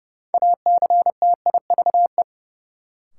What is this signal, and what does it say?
Morse code, heard as a keyed tone, about 30 wpm, 700 Hz. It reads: ACTIVE
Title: Sentences from Top 900 Words - New Word Drill - 30wpm